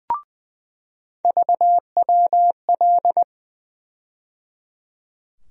Morse code, heard as a keyed tone, about 20 wpm, 700 Hz. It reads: VWL
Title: USA Call Sign Postfixes - 3 Letters 20wpm